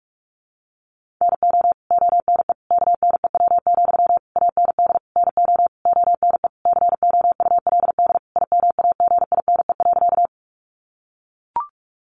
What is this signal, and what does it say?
Morse code, heard as a keyed tone, about 44 wpm, 700 Hz. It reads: NO ONE KNEW, AND NO ONE COULD IMAGINE.
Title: Sentences from Top 1400 Words 45wpm